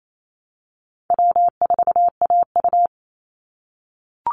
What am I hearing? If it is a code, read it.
Morse code, heard as a keyed tone, about 28 wpm, 700 Hz. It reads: W4AU
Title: Easy Call Signs - 28wpm